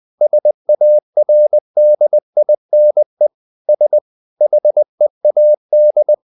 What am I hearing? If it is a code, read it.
Morse code, heard as a keyed tone, about 20 wpm, 600 Hz. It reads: SARDINE S HEAD
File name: the old man and the sea / the old man and the sea part 1 20wpm